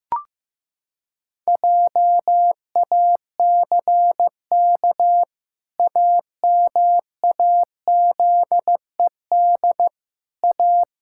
Morse code, heard as a keyed tone, about 15 wpm, 700 Hz. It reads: JACK AMAZED A
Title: Pangram Challenge v2 - 15wpm